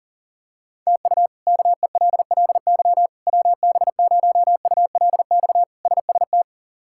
Morse code, heard as a keyed tone, about 37 wpm, 700 Hz. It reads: TU KELLY WB0ULX SST
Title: K1USN SST QSO Practice - 40wpm